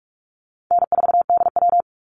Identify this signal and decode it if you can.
Morse code, heard as a keyed tone, about 44 wpm, 700 Hz. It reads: N4DW
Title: Easy Call Signs Repeated 2x - 45wpm